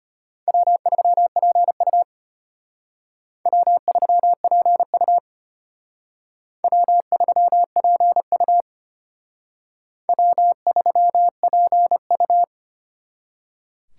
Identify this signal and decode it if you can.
Morse code, heard as a keyed tone, about 32 wpm, 700 Hz. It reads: W3PU W3PU W3PU W3PU